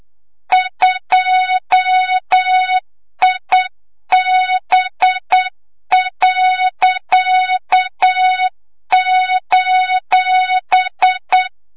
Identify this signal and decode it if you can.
Morse code, heard as a keyed tone, about 8 wpm, 3700 Hz, with noise. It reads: 2IB.:
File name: Morze-HA2MN / hang / 25 GYAKORLAT